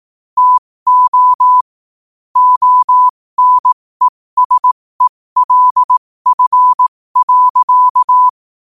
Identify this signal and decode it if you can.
Morse code, heard as a keyed tone, about 18 wpm, spaced as for 15 wpm, 1000 Hz. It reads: TO ONESELF.
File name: qotdfiles / 15wpm